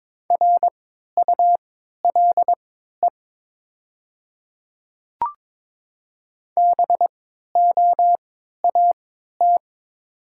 Morse code, heard as a keyed tone, about 22 wpm, spaced as for 13 wpm, 700 Hz. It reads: RULE BOAT
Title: 3x Character Spacing - Top 500 Words - 22wpm